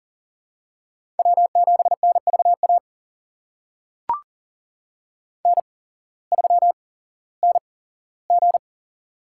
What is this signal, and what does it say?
Morse code, heard as a keyed tone, about 37 wpm, spaced as for 18 wpm, 700 Hz. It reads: W7NVA N3NG
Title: USA Call Signs - 8x Character Spacing - 40wpm